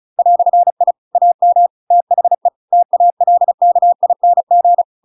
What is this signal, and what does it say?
Morse code, heard as a keyed tone, about 34 wpm, 700 Hz. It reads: "I AM THE TALKING